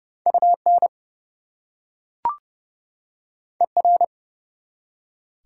Morse code, heard as a keyed tone, about 29 wpm, 700 Hz. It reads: UN ER
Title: Prefixes and Suffixes 30wpm